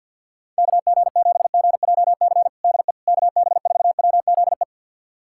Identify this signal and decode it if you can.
Morse code, heard as a keyed tone, about 47 wpm, 690 Hz. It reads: KG7GJC DE KB4WZE
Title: USA Callsign DE Callsign 50wpm